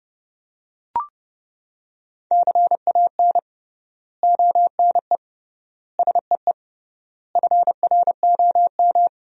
Morse code, heard as a keyed tone, about 30 wpm, 700 Hz. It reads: CAN ONE SEE FROM
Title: 3x Word-Spacing - Sentences from Top 100 Words - 30wpm